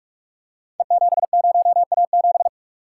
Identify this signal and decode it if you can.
Morse code, heard as a keyed tone, about 44 wpm, 700 Hz. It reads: E70A7
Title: Code Groups - 5 Letter:Numbers - 45wpm